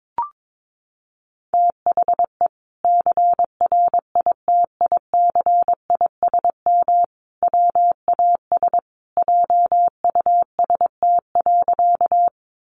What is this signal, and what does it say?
Morse code, heard as a keyed tone, about 22 wpm, 700 Hz. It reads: THE CRITICISM WAS JUST.
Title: Sentences from Top 1300 Words 22wpm